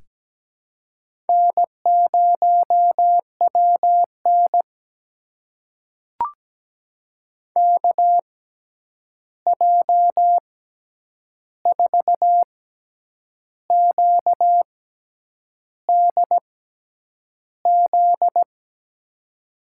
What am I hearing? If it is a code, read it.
Morse code, heard as a keyed tone, about 17 wpm, spaced as for 8 wpm, 700 Hz. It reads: N0WN KJ4QDZ
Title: USA Call Signs - 6x Character Spacing - 17wpm